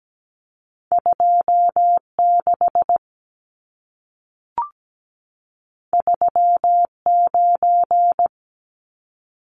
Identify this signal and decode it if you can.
Morse code, heard as a keyed tone, about 17 wpm, 700 Hz. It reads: 26 39
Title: Numbers - 2-Digit 17wpm